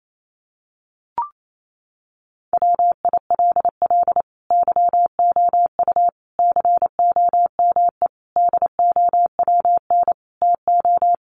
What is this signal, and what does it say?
Morse code, heard as a keyed tone, about 28 wpm, 700 Hz. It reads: WILL YOU COME DOWN TO
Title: Sentences from Top 1500 Words 28wpm